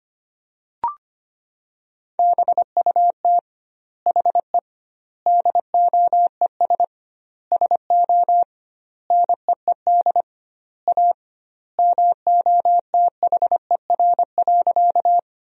Morse code, heard as a keyed tone, about 25 wpm, 700 Hz. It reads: BUT HE DOES SO NEED A MOTHER.
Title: 2x Word-Spacing - Sentences from Top 200 Words - 25wpm